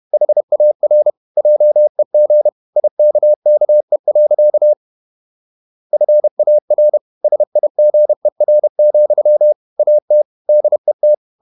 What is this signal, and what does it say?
Morse code, heard as a keyed tone, about 31 wpm, 600 Hz. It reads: HAR JEG IKKE. FAR SIGER, AT DET
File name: Ferie QRQ 31wpm